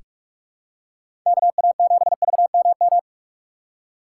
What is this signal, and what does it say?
Morse code, heard as a keyed tone, about 46 wpm, 700 Hz. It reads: KA7VMM
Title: USA Call Signs - 6 Characters - 6x Character Spacing - 45wpm